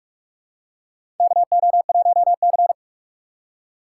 Morse code, heard as a keyed tone, about 42 wpm, 700 Hz. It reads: KO1C
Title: USA Call Signs - 4 Characters - 6x Character Spacing - 45wpm